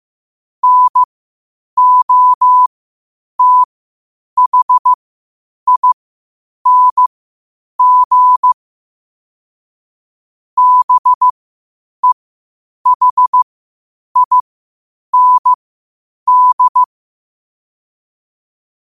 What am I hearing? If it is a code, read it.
Morse code, heard as a keyed tone, about 15 wpm, spaced as for 8 wpm, 1000 Hz. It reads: NOTHING BEHIND